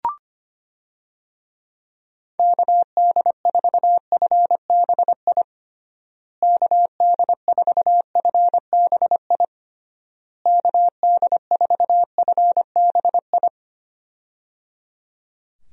Morse code, heard as a keyed tone, about 25 wpm, 700 Hz. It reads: KD4FBI KD4FBI KD4FBI
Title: USA Call Signs - 6 Characters - Repeated 3x - 25wpm